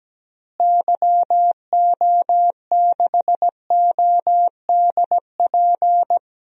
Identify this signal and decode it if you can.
Morse code, heard as a keyed tone, about 17 wpm, 700 Hz. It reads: YO6ODP
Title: DXCC-100 Call Signs - 6 Characters - Repeated 2x - 17wpm